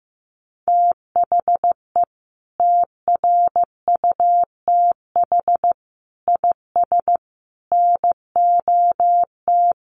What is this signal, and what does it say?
Morse code, heard as a keyed tone, about 15 wpm, 700 Hz. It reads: THE TRUTH IS NOT
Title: Sentences from Top 1500 Words 15wpm